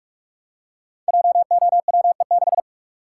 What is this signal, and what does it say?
Morse code, heard as a keyed tone, about 44 wpm, 700 Hz. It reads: JOWE6
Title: Code Groups - 5 Letter:Numbers - 45wpm